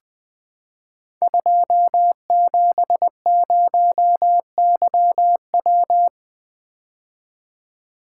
Morse code, heard as a keyed tone, about 20 wpm, 700 Hz. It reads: 270YW
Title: Code Groups - 5 Letter:Numbers - 20wpm